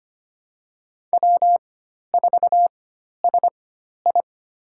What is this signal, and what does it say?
Morse code, heard as a keyed tone, about 25 wpm, spaced as for 12 wpm, 700 Hz. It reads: W4SI
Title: USA Call Signs - 4 Characters - 4x Character Spacing - 25wpm